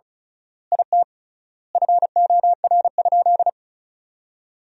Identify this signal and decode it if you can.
Morse code, heard as a keyed tone, about 34 wpm, 700 Hz. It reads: IT FOR?
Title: 3x Word-Spacing - Sentences from Top 100 Words - 35wpm